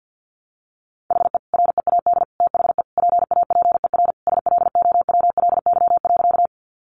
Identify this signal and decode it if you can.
Morse code, heard as a keyed tone, about 50 wpm, 700 Hz. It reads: HE READ THE PAPER SLOWLY.
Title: Sentences from Top 1100 Words 50wpm